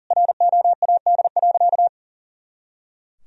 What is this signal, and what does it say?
Morse code, heard as a keyed tone, about 40 wpm, 700 Hz. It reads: ROAD.
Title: Sentences from Top 400 Words and Abbreviations 40wpm